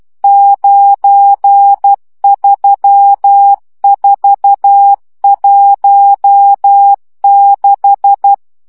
Morse code, heard as a keyed tone, about 12 wpm, 800 Hz. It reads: 93416
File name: morze vizsga 2011-3 10